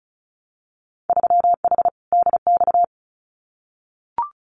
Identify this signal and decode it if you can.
Morse code, heard as a keyed tone, about 34 wpm, 700 Hz. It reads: 3H DX
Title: Winter Field Day Exchanges Encore 35wpm